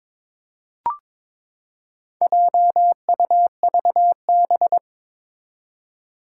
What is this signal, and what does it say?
Morse code, heard as a keyed tone, about 22 wpm, 700 Hz. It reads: JUVB
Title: Four Challenging Letter-Number Combinations 22wpm